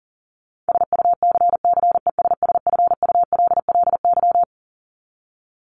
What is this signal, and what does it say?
Morse code, heard as a keyed tone, about 40 wpm, 700 Hz. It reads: SUCCESSFULLY
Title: Top 1501-1750 Words 40wpm